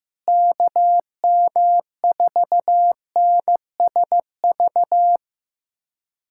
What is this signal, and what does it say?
Morse code, heard as a keyed tone, about 15 wpm, 700 Hz. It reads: KM4NSV
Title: USA Callsign DE Callsign - 6 Characters - 15wpm